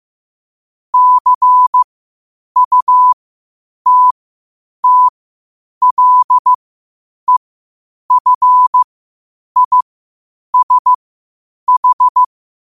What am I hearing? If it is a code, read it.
Morse code, heard as a keyed tone, about 15 wpm, spaced as for 8 wpm, 1000 Hz. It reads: CUTTLEFISH